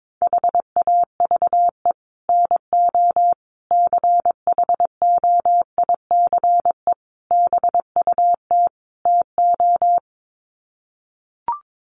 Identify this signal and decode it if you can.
Morse code, heard as a keyed tone, about 22 wpm, 700 Hz. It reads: HAVE NO CHOICE BUT TO
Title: Sets of 5 Words Encore 22wpm